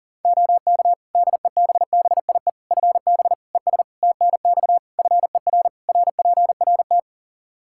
Morse code, heard as a keyed tone, about 37 wpm, 700 Hz. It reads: OK DEBBIE FB ES TNX FER RPRT
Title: Practice QSO - Exchange 2 - Info - 40wpm